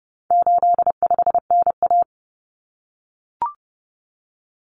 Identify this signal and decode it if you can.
Morse code, heard as a keyed tone, about 30 wpm, 700 Hz. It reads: 85NA
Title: Common Cut Numbers - 4-Digit 30wpm